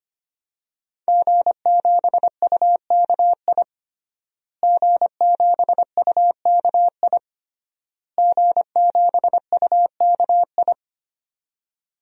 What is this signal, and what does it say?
Morse code, heard as a keyed tone, about 25 wpm, 700 Hz. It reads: G7UKI G7UKI G7UKI
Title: DXCC-100 Call Signs - 5 Characters - Repeated 3x - 25wpm